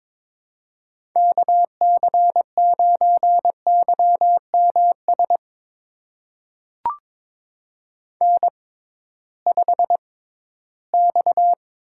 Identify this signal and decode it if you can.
Morse code, heard as a keyed tone, about 23 wpm, spaced as for 14 wpm, 700 Hz. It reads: KC9YMS N5X
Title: USA Call Signs - 6x Character Spacing - 22wpm